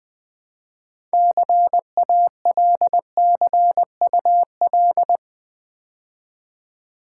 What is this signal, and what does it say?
Morse code, heard as a keyed tone, about 20 wpm, 700 Hz. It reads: CALCUL
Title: Build A Word - 3 Letters At A Time - 9-Letter Words - 20wpm